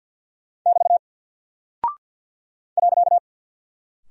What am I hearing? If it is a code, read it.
Morse code, heard as a keyed tone, about 48 wpm, 700 Hz. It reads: <BT> .
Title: Single Character (Letter:Number:Punctuation:Prosign) Alt 50wpm